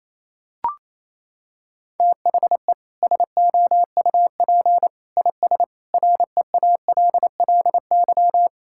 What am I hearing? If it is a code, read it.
Morse code, heard as a keyed tone, about 28 wpm, 700 Hz. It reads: THE SOUP IS REALLY
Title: Sentences from Top 1000 Words - Review Part 6 - 28wpm